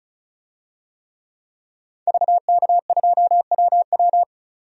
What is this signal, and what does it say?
Morse code, heard as a keyed tone, about 34 wpm, 700 Hz. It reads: VK2WW
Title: DXCC-100 Call Signs - 5 Characters - 35wpm